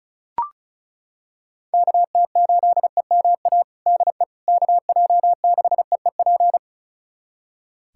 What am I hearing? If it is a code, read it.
Morse code, heard as a keyed tone, about 36 wpm, 700 Hz. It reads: KT8EMA DE KJ6EEP
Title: USA Callsign DE Callsign 35wpm